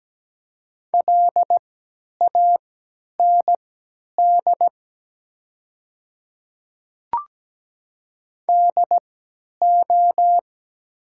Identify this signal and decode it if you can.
Morse code, heard as a keyed tone, about 17 wpm, spaced as for 10 wpm, 700 Hz. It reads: LAND DO
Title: 3x Character Spacing - Top 200 Words - 17wpm